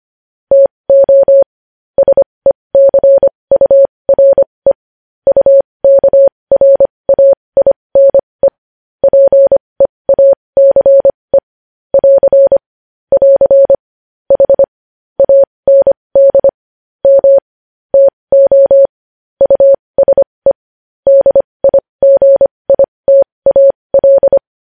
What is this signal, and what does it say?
Morse code, heard as a keyed tone, about 25 wpm, spaced as for 20 wpm, 550 Hz. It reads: TO SECURE UKRAINE PEACE <AR> <AR> H AND M TO USE DIGITAL
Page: news-headlines-in-morse-code-at-20-wpm